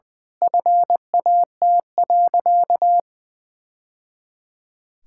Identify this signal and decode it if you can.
Morse code, heard as a keyed tone, about 20 wpm, 700 Hz. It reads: FAT.
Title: Sentences from Top 1000 Words - Review Part 5 - 20wpm